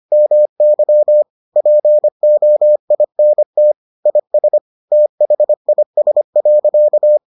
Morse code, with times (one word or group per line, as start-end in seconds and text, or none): 0.1-1.2 MY
1.6-3.7 POINT
4.1-4.6 IS
4.9-7.2 THIS.